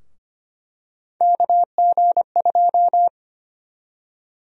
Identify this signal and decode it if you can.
Morse code, heard as a keyed tone, about 25 wpm, 700 Hz. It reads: KG2
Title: USA Call Sign Prefixes - Repeated 2x - 25wpm